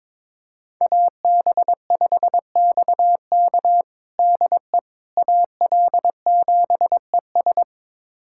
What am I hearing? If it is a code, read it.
Morse code, heard as a keyed tone, about 21 wpm, 700 Hz. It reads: AB5XK DE AL7ES
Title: USA Callsign DE Callsign - 5 Characters - 22wpm